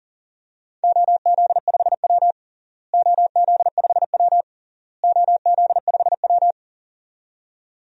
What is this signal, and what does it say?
Morse code, heard as a keyed tone, about 40 wpm, 700 Hz. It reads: OZ5W OZ5W OZ5W
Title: DXCC-100 Call Signs - 4 Characters - Repeated 3x - 40wpm